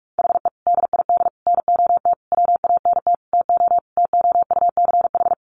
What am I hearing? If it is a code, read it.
Morse code, heard as a keyed tone, about 44 wpm, 700 Hz. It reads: HE DID NOT WANT TO TOUCH
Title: Sentences from Top 1200 Words 45wpm